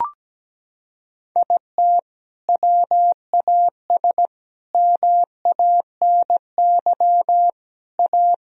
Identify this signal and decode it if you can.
Morse code, heard as a keyed tone, about 17 wpm, 700 Hz. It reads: IT WAS MANY A
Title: Sentences from Top 1000 Words - Review Part 1 - 17wpm